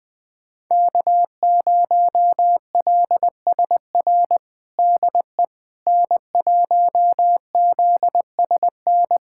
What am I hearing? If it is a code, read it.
Morse code, heard as a keyed tone, about 20 wpm, 700 Hz. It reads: K0LSR DE N1ZSN
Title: USA Callsign DE Callsign - 5 Characters - 20wpm